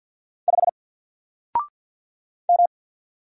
Morse code, heard as a keyed tone, about 50 wpm, 700 Hz. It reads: 5 M
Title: Single Letter-Number 50wpm